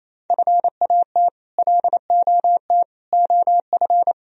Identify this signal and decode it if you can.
Morse code, heard as a keyed tone, about 27 wpm, 700 Hz. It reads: FAT LOT OF
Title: Sentences from Top 1000 Words - Review Part 4 - 28wpm